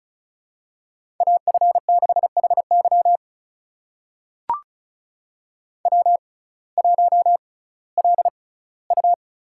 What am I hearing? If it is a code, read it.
Morse code, heard as a keyed tone, about 34 wpm, spaced as for 19 wpm, 700 Hz. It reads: AF6HY W1LU
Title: USA Call Signs - 5 Characters - 6x Character Spacing - 35wpm